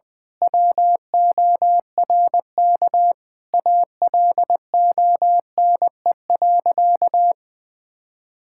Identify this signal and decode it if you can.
Morse code, heard as a keyed tone, about 19 wpm, 700 Hz. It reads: WORK ALONE.
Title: Sentences from Top 1600 Words 20wpm